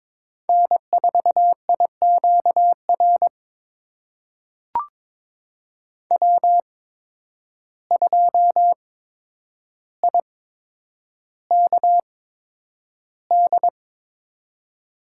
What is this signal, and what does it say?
Morse code, heard as a keyed tone, about 22 wpm, spaced as for 9 wpm, 700 Hz. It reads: N4IQR W2IKD
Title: USA Call Signs - 5 Characters - 8x Character Spacing - 22wpm